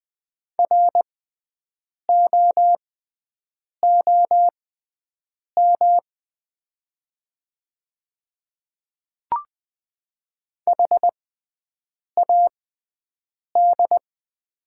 Morse code, heard as a keyed tone, about 20 wpm, spaced as for 7 wpm, 700 Hz. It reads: ROOM HAD
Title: 6x Character Spacing - Top 300 Words - 20wpm